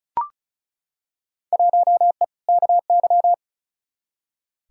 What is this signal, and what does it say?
Morse code, heard as a keyed tone, about 36 wpm, 700 Hz. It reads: 1E KY
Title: Field Day Exchanges - 35wpm